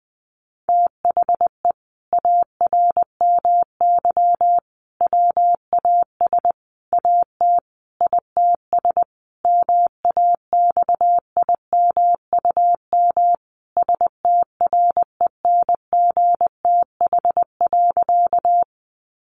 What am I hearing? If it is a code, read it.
Morse code, heard as a keyed tone, about 20 wpm, 700 Hz. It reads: THE ARMY WAS AT ITS MAXIMUM STRENGTH.
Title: Sentences from Top 1300 Words 20wpm